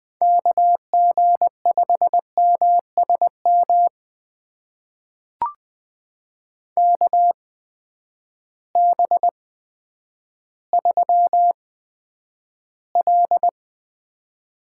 Morse code, heard as a keyed tone, about 20 wpm, spaced as for 10 wpm, 700 Hz. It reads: KG5MSM KB3L
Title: USA Call Signs - 6 Characters - 8x Character Spacing - 20wpm